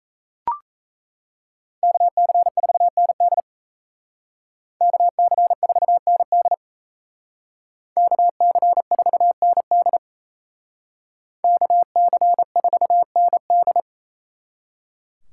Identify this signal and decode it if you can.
Morse code, heard as a keyed tone, about 35 wpm, 700 Hz. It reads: KC4ND KC4ND KC4ND KC4ND